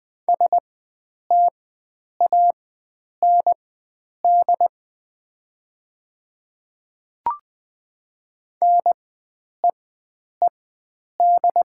Morse code, heard as a keyed tone, about 20 wpm, spaced as for 9 wpm, 700 Hz. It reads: STAND NEED
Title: 4x Character Spacing - Top 300 Words - 20wpm